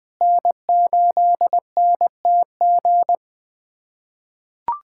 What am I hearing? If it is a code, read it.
Morse code, heard as a keyed tone, about 20 wpm, 700 Hz. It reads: N8NTG
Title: USA Callsign DE Callsign - 5 Characters - 20wpm